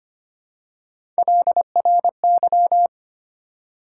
Morse code, heard as a keyed tone, about 25 wpm, 700 Hz. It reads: LRY
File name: Three Letter-Number Combinations - 25wpm